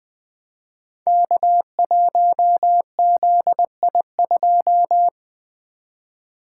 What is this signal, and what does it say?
Morse code, heard as a keyed tone, about 20 wpm, 700 Hz. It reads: K1ZI2
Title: Code Groups - 5 Letter:Numbers - 20wpm